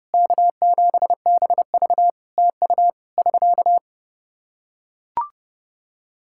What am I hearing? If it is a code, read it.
Morse code, heard as a keyed tone, about 28 wpm, 700 Hz. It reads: K7BV TU <SK>
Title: Practice QSO - Exchange 4 - Ending - 30wpm